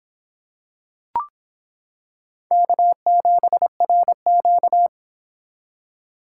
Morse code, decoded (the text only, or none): K7RQ